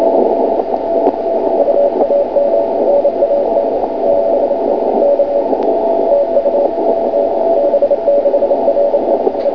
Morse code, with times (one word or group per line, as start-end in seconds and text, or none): none